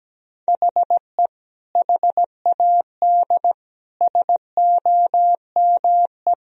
Morse code, heard as a keyed tone, about 17 wpm, 700 Hz. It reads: HE HAD SOME
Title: Sentences from Top 1000 Words - Review Part 3 - 17wpm